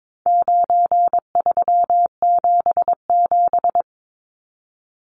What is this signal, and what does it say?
Morse code, heard as a keyed tone, about 22 wpm, 700 Hz. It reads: 9377
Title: Numbers - 4-Digit 22wpm